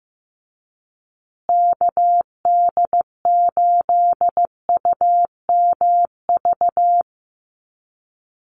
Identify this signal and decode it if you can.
Morse code, heard as a keyed tone, about 15 wpm, 700 Hz. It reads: KD8UMV
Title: USA Call Signs Repeated 2x 15wpm